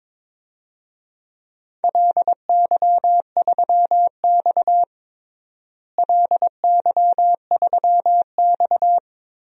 Morse code, heard as a keyed tone, about 22 wpm, 700 Hz. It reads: LY3X LY3X
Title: DXCC-100 Call Signs - 4 Characters - Repeated 3x - 22wpm